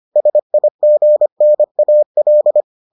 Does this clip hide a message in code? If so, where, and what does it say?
Yes: Morse code from 0.2 to 2.6 s, heard as a keyed tone, about 25 wpm, 600 Hz. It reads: SIGNAL